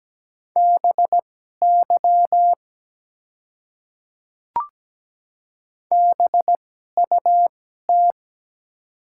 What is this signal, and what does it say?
Morse code, heard as a keyed tone, about 17 wpm, spaced as for 12 wpm, 700 Hz. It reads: BY BUT
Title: 2x Character Spacing - Top 100 Words - 17wpm